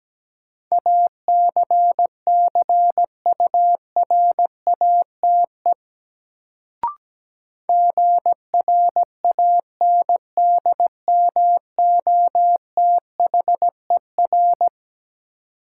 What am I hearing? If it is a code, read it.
Morse code, heard as a keyed tone, about 17 wpm, 700 Hz. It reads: ACCURATE GRANDMOTHER
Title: Sentences from Top 1200 Words - New Word Drill - 17wpm